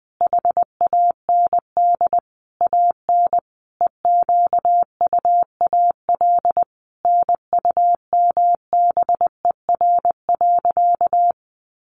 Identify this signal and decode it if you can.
Morse code, heard as a keyed tone, about 20 wpm, 700 Hz. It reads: HAND AN EQUAL NUMBER.
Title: Sentences from Top 1400 Words 20wpm